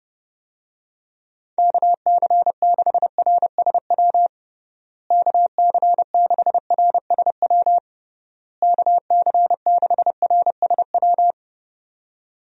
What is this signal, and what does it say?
Morse code, heard as a keyed tone, about 32 wpm, 700 Hz. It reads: KC6RSW KC6RSW KC6RSW
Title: USA Call Signs - 6 Characters - Repeated 3x - 30wpm